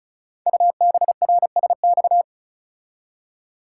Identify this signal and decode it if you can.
Morse code, heard as a keyed tone, about 36 wpm, 700 Hz. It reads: UBRSX
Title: Code Groups - 5 Letters - 35wpm